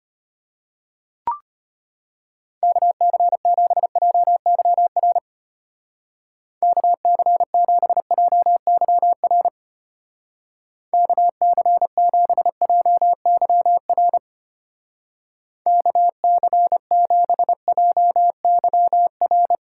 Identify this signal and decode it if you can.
Morse code, heard as a keyed tone, about 31 wpm, 700 Hz. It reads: KC7JYR KC7JYR KC7JYR KC7JYR
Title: Speed Racing - USA 2x3 Call Signs 25wpm